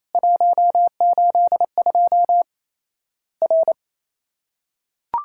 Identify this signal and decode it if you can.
Morse code, heard as a keyed tone, about 27 wpm, 700 Hz. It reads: 182 R
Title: Worked All Europe DX Contest - QTC Exchange - 28wpm